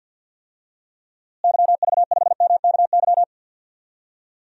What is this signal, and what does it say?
Morse code, heard as a keyed tone, about 46 wpm, 700 Hz. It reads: YV5MKY
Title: DXCC-100 Call Signs - 6 Characters - Repeated 3x - 50wpm